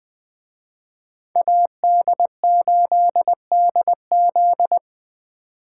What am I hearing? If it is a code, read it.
Morse code, heard as a keyed tone, about 20 wpm, 700 Hz. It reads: AD8DZ
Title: Speed Racing - USA 2x2 Call Signs 17wpm